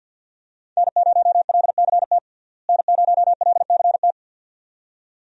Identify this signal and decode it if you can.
Morse code, heard as a keyed tone, about 49 wpm, 700 Hz. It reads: N0LCT N0LCT
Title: USA Call Signs - 5 Characters - Repeated 2x - 50wpm